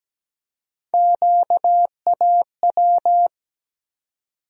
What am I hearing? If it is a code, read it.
Morse code, heard as a keyed tone, about 17 wpm, 700 Hz. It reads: QAW
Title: USA Call Sign Postfixes - 3 Letters 17wpm